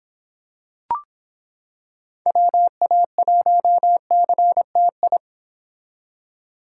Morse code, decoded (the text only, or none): WA1CTI